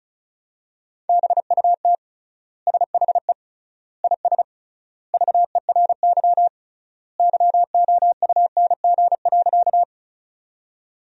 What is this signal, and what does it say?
Morse code, heard as a keyed tone, about 36 wpm, 700 Hz. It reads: BUT SHE IS VERY YOUNG.
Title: 3x Word-Spacing - Sentences from Top 400 Words - 35wpm